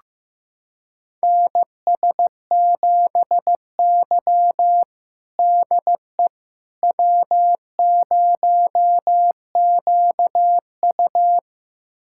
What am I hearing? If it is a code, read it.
Morse code, heard as a keyed tone, about 16 wpm, 700 Hz. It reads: NS7Y DE W0QU
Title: USA Callsign DE Callsign - 4 Characters - 15wpm